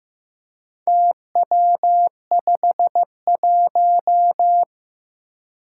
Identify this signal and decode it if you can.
Morse code, heard as a keyed tone, about 15 wpm, 700 Hz. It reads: TW51
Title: Four Challenging Letter-Number Combinations 15wpm